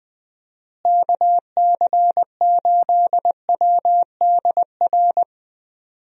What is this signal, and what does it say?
Morse code, heard as a keyed tone, about 20 wpm, 700 Hz. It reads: KC8WDR
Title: USA Call Signs - 6 Characters - Repeated 2x - 20wpm